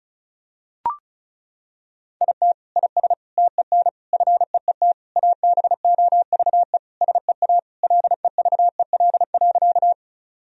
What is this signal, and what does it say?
Morse code, heard as a keyed tone, about 36 wpm, 700 Hz. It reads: IT IS TEN FEET ABOVE SEA LEVEL.
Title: Sentences from Top 1000 Words - Review Part 4 - 35wpm